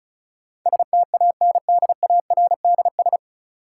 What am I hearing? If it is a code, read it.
Morse code, heard as a keyed tone, about 36 wpm, 700 Hz. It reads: STANDARDS
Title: Sentences from Top 1000 Words - Review Part 6 - New Word Drill - 35wpm